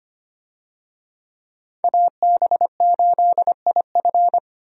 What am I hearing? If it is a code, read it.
Morse code, heard as a keyed tone, about 25 wpm, 700 Hz. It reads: AB8IF
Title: USA Call Signs - 5 Characters - Repeated 2x - 25wpm